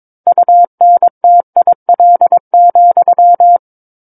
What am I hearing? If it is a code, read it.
Morse code, heard as a keyed tone, about 22 wpm, 700 Hz. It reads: UNTIL,
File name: Book One / 13-Chapter 13 - 22 WPM